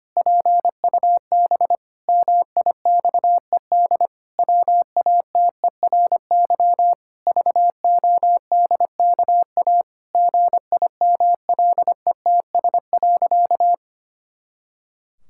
Morse code, heard as a keyed tone, about 25 wpm, 700 Hz. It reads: PUB MIXED WATERY VODKA GIMLETS.
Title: Pangram Challenge v2 - 25wpm